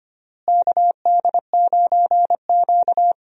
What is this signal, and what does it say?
Morse code, heard as a keyed tone, about 25 wpm, 700 Hz. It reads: KD9Q
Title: USA Call Signs - 4 Characters - Repeated 3x - 25wpm